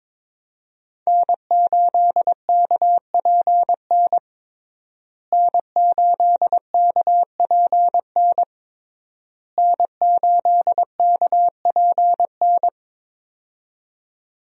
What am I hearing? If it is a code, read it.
Morse code, heard as a keyed tone, about 22 wpm, 700 Hz. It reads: N8KPN N8KPN N8KPN
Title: USA Call Signs - 5 Characters - Repeated 3x - 22wpm